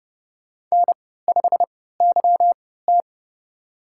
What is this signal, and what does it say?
Morse code, heard as a keyed tone, about 32 wpm, spaced as for 17 wpm, 700 Hz. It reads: N5YT
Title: USA Call Signs - 4 Characters - 3x Character Spacing - 30wpm